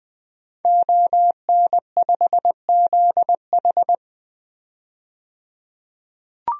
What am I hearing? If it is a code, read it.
Morse code, heard as a keyed tone, about 20 wpm, 700 Hz. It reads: ON5ZH
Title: DXCC-100 Call Signs - 5 Characters - 20wpm